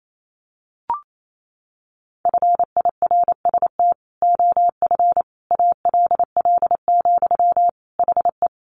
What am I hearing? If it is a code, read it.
Morse code, heard as a keyed tone, about 28 wpm, 700 Hz. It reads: FIRST OF ALL, HE
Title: Sentences from Top 1200 Words 28wpm